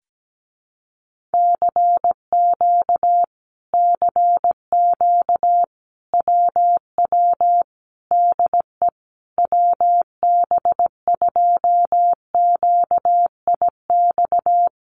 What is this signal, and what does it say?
Morse code, heard as a keyed tone, about 17 wpm, 700 Hz. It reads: CQ CQ WW DE WB2QIX
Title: Common QSO Phrases Encore 17wpm